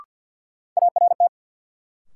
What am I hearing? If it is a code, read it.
Morse code, heard as a keyed tone, about 51 wpm, 700 Hz. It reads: ART
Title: Sentences from Top 600 Words - New Word Drill - 50wpm